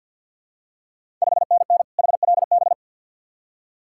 Morse code, heard as a keyed tone, about 47 wpm, 690 Hz. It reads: 5NN HLB
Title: Ontario QSO Party Exchange 50wpm